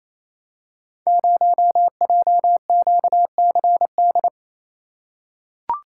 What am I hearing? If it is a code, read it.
Morse code, heard as a keyed tone, about 28 wpm, 700 Hz. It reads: 0JQCD
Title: Code Groups - 5 Letter:Numbers - 28wpm